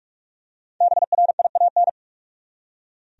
Morse code, heard as a keyed tone, about 46 wpm, 700 Hz. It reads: BRIAN
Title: USA Common Names - Max 5 Letters - 45wpm